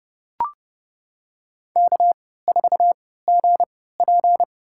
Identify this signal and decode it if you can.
Morse code, heard as a keyed tone, about 32 wpm, spaced as for 17 wpm, 700 Hz. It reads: K4GP
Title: USA Call Signs - 4 Characters - 3x Character Spacing - 30wpm